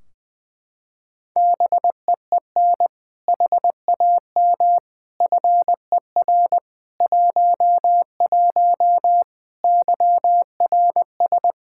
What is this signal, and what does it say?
Morse code, heard as a keyed tone, about 21 wpm, 700 Hz. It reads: BEEN HAM FER 11 YRS
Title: Practice QSO - Exchange 3 - More Info - 20wpm